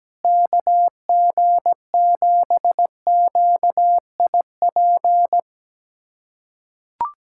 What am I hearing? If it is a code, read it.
Morse code, heard as a keyed tone, about 17 wpm, 700 Hz. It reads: KG7QIP
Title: USA Callsign DE Callsign 17wpm